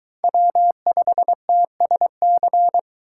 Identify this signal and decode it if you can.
Morse code, heard as a keyed tone, about 22 wpm, 700 Hz. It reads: W5TSC